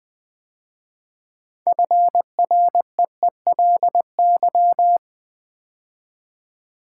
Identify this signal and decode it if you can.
Morse code, heard as a keyed tone, about 20 wpm, 700 Hz. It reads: FREELY
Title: 6 Letter Words - First 3 Letters Given - 20wpm